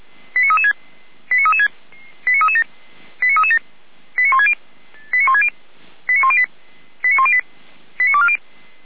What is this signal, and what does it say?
Five-tone selective call (ZVEI), heard as quick runs of five tones, tones of about 70 ms. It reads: 89336 89336 89337 89337 89160 89160 89108 89108 89240